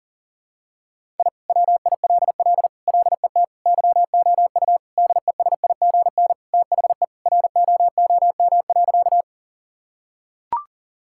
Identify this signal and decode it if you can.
Morse code, heard as a keyed tone, about 40 wpm, 700 Hz. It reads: I WILL LET YOU DESIGN THE ROOM.
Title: Sentences from Top 1000 Words - Review Part 5 - 40wpm